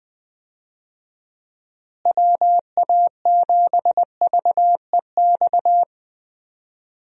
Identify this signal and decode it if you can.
Morse code, heard as a keyed tone, about 20 wpm, 700 Hz. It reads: WA7VEX
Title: USA Call Signs - 6 Characters - Repeated 2x - 20wpm